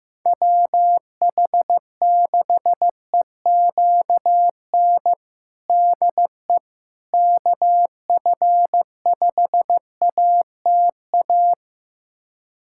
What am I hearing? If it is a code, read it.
Morse code, heard as a keyed tone, about 15 wpm, 700 Hz. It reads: WH6EQN DE KF5ATA
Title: USA Callsign DE Callsign - 6 Characters - 15wpm